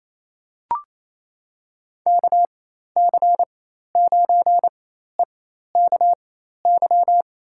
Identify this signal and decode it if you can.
Morse code, heard as a keyed tone, about 28 wpm, spaced as for 13 wpm, 700 Hz. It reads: KC9EKY
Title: USA Call Signs - 6 Characters - 4x Character Spacing - 28wpm